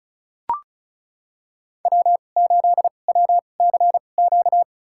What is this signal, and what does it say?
Morse code, heard as a keyed tone, about 36 wpm, spaced as for 25 wpm, 700 Hz. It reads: W8WCQ
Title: USA Call Signs - 5 Characters - 2x Character Spacing - 35wpm